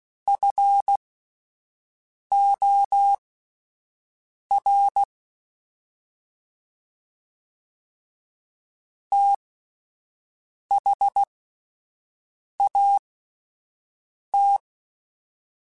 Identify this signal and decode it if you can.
Morse code, heard as a keyed tone, about 16 wpm, spaced as for 5 wpm, 800 Hz. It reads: FOR THAT
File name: (spaced) / QSOs Spaced-mp3s / WA7QOX 1 @10 sp